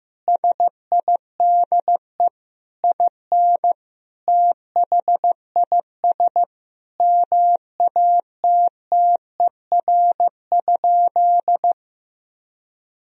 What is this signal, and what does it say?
Morse code, heard as a keyed tone, about 16 wpm, 700 Hz. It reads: SIDE IN THIS MATTER?
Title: Sentences from Top 1000 Words - Review Part 1 - 15wpm